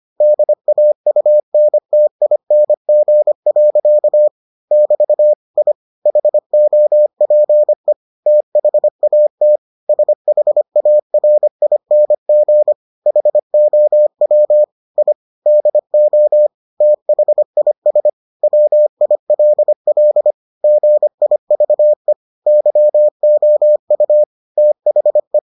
Morse code, heard as a keyed tone, about 25 wpm, 600 Hz. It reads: DAUNTING. <BT> I HOPE THAT SHARING HOW I DO THIS WILL GIVE YOU THE